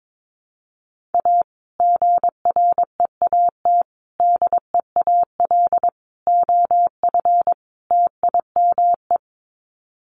Morse code, heard as a keyed tone, about 22 wpm, 700 Hz. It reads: A GREAT DEAL OF TIME
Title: Sets of 5 Words Encore2 22wpm